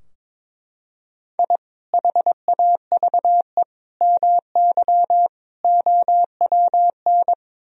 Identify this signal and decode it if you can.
Morse code, heard as a keyed tone, about 21 wpm, 700 Hz. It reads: I HAVE MY OWN
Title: Sentences from Top 1000 Words - Review Part 3 - 22wpm